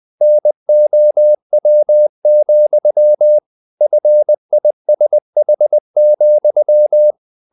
Morse code, heard as a keyed tone, about 20 wpm, 600 Hz. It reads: NOW, FISH,
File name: the old man and the sea / the old man and the sea part 1 20wpm